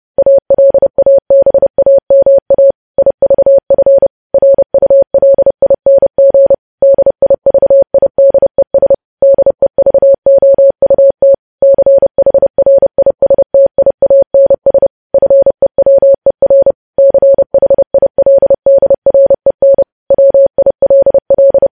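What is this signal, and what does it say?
Morse code, heard as a keyed tone, about 30 wpm, 550 Hz. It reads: ALABAMA IVF RULING DIVIDES DEVOUT CHRISTIANS FEWER CHILDREN WILL
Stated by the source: Morse news headlines sent at 30 wpm, opening with vvv.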